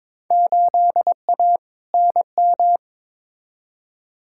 Morse code, heard as a keyed tone, about 22 wpm, 700 Hz. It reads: 8A NM
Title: Field Day Exchanges Encore - 22wpm